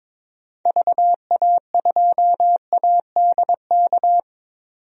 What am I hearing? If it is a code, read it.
Morse code, heard as a keyed tone, about 22 wpm, 700 Hz. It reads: VA2ADK
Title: DXCC-100 Call Signs - 6 Characters - 22wpm